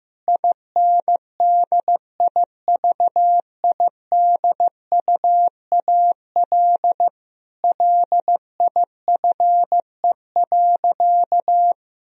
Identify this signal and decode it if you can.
Morse code, heard as a keyed tone, about 15 wpm, 700 Hz. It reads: INDIVIDUAL LIFE.
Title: Sentences from Top 1000 Words - Review Part 4 - 15wpm